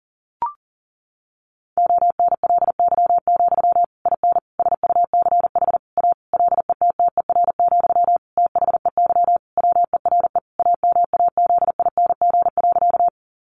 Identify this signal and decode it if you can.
Morse code, heard as a keyed tone, about 37 wpm, 700 Hz. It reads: ONLY, IN SUCH A LETTER, THEY WERE AMAZING.
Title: Sentences from Top 1500 Words 40wpm